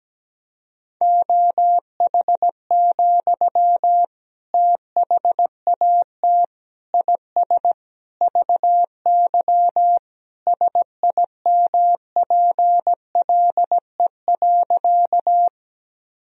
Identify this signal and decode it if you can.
Morse code, heard as a keyed tone, about 17 wpm, 700 Hz. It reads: OH, THAT IS VY SIMPLE.
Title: Sentences from Top 500 Words and Abbreviations 17wpm